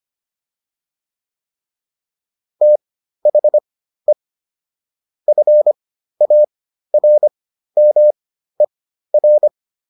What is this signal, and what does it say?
Morse code, heard as a keyed tone, about 25 wpm, spaced as for 13 wpm, 600 Hz. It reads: THE FARMER